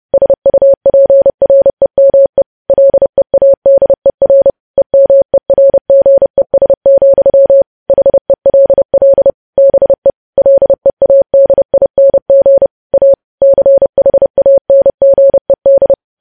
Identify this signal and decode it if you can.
Morse code, heard as a keyed tone, about 30 wpm, 550 Hz. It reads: SUPREME LEADER EMERGES, HELL BE LEADING A CHANGED
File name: podcast / News headlines-1x-30x30